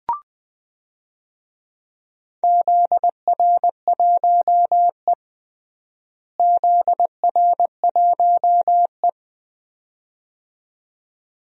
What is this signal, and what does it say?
Morse code, heard as a keyed tone, about 20 wpm, 700 Hz. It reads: ZR1E ZR1E
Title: DXCC-100 Call Signs - 4 Characters - Repeated 2x - 20wpm